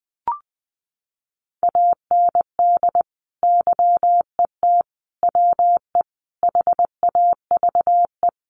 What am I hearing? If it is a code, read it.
Morse code, heard as a keyed tone, about 20 wpm, 700 Hz. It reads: AND YET WE HAVE
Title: Sentences from Top 1500 Words 20wpm